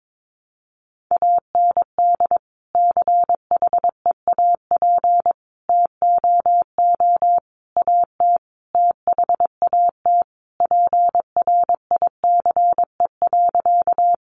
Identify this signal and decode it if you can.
Morse code, heard as a keyed tone, about 22 wpm, 700 Hz. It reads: AND CHEAP TOO AT THAT PRICE.
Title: Sentences from Top 1200 Words 22wpm